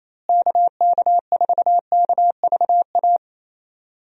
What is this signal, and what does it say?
Morse code, heard as a keyed tone, about 28 wpm, 700 Hz. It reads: KK4KVA
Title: USA Call Signs - 6 Characters - Repeated 2x - 28wpm